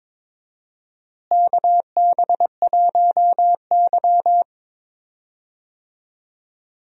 Morse code, heard as a keyed tone, about 22 wpm, 700 Hz. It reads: KB1Y
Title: USA Call Signs - 4 Characters - Repeated 3x - 22wpm